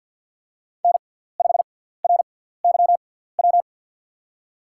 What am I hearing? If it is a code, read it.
Morse code, heard as a keyed tone, about 52 wpm, spaced as for 17 wpm, 690 Hz. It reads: N5RYW
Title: USA Call Signs - 6x Character Spacing - 50wpm